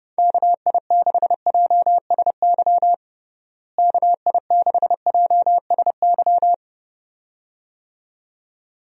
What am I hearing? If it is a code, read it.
Morse code, heard as a keyed tone, about 30 wpm, 700 Hz. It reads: KI6JSY KI6JSY
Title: USA Call Signs - 6 Characters - Repeated 2x - 30wpm